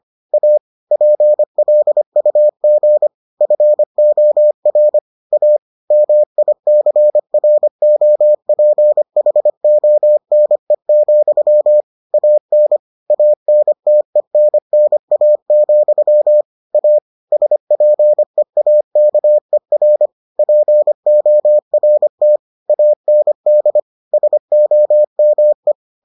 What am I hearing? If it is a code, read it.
Morse code, heard as a keyed tone, about 25 wpm, 600 Hz. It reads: A PLUG FOR A MICROPHONE, AN ANTENNA, A SPEAKER PORT AND SOME